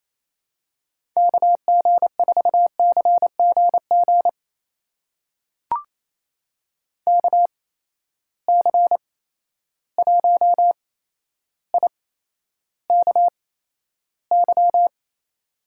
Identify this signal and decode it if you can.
Morse code, heard as a keyed tone, about 28 wpm, spaced as for 12 wpm, 700 Hz. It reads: KG4CGG KC1IKY
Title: USA Call Signs - 6 Characters - 8x Character Spacing - 28wpm